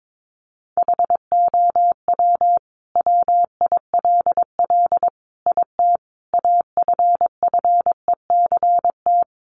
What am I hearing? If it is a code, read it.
Morse code, heard as a keyed tone, about 22 wpm, 700 Hz. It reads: HOW WILL IT AFFECT
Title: Sentences from Top 900 Words 22wpm